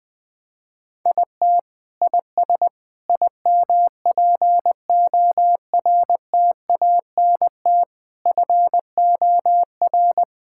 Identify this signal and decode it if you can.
Morse code, heard as a keyed tone, about 20 wpm, 700 Hz. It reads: IT IS IMPORTANT FOR
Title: Sentences with 4-Word Phrases - 20wpm